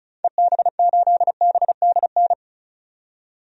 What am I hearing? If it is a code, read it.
Morse code, heard as a keyed tone, about 34 wpm, 700 Hz. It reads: EB8BDN
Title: DXCC-100 Call Signs - 6 Characters - 35wpm